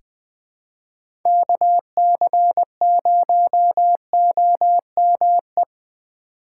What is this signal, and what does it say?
Morse code, heard as a keyed tone, about 21 wpm, 700 Hz. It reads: KC0OME
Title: USA Call Signs - 6 Characters - 4x Character Spacing - 20wpm